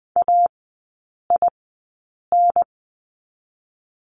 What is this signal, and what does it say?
Morse code, heard as a keyed tone, about 20 wpm, 700 Hz. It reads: A I N
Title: 029 Lesson 6e - Letter i - Three Characters 20wpm